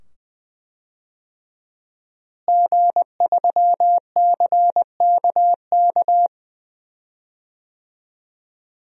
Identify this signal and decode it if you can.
Morse code, heard as a keyed tone, about 20 wpm, 700 Hz. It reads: G3CKK
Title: DXCC-100 Call Signs - 5 Characters - Repeated 2x - 20wpm